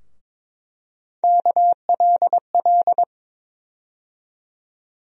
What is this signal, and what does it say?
Morse code, heard as a keyed tone, about 22 wpm, 700 Hz. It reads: KLL
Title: USA Call Sign Postfixes - 3 Letters 22wpm